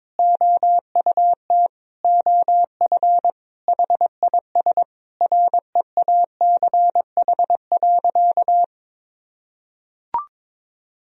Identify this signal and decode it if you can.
Morse code, heard as a keyed tone, about 21 wpm, 700 Hz. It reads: OUT OF HIS REACH.
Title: Sentences from Top 1000 Words - Review Part 2 - 22wpm